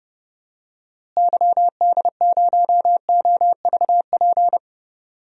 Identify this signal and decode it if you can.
Morse code, heard as a keyed tone, about 31 wpm, 700 Hz. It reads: YD0OVP
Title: DXCC-100 Call Signs - 6 Characters - Repeated 3x - 30wpm